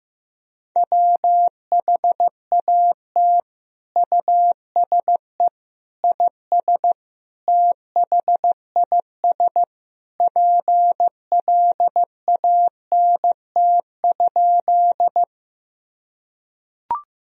Morse code, heard as a keyed tone, about 15 wpm, 700 Hz. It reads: WHAT USE IS THIS PLANT?
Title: Sentences from Top 1000 Words - Review Part 1 - 15wpm